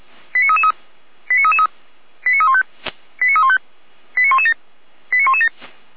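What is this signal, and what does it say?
Five-tone selective call (ZVEI), heard as quick runs of five tones, tones of about 70 ms. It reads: 89333 89333 89315 89315 89117 89117
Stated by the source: ZVEI five-tone call 89509 received by SDR, from audio.f85360.feuerwehr-orlamuende.org fire brigade (Feuerwehr Orlamünde) archive.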